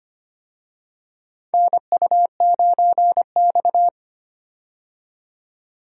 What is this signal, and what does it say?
Morse code, heard as a keyed tone, about 25 wpm, 700 Hz. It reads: NU9X
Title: USA Call Signs - 4 Characters - 25wpm